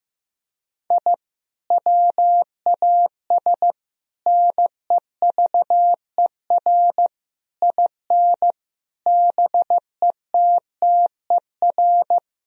Morse code, heard as a keyed tone, about 15 wpm, 700 Hz. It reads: I WAS NEVER IN BETTER
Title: Sentences from Top 1000 Words - Review Part 3 - 15wpm